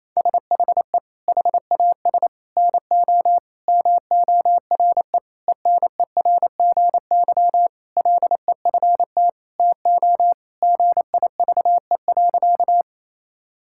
Morse code, heard as a keyed tone, about 27 wpm, 700 Hz. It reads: SHE HAS NO MORE ENERGY LEFT TO GIVE.
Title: Sentences from Top 1000 Words - Review Part 6 - 28wpm